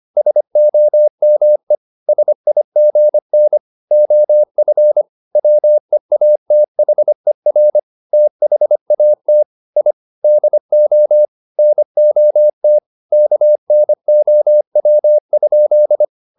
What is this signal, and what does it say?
Morse code, heard as a keyed tone, about 25 wpm, 600 Hz. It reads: SOME SIGN OF WEATHER THAT I DO NOT KNOW?
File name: the old man and the sea part 1 25wpm